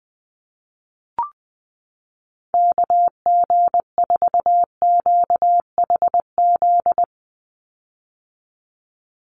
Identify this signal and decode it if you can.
Morse code, heard as a keyed tone, about 20 wpm, 700 Hz. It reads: KG4QHZ
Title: USA Call Signs 20wpm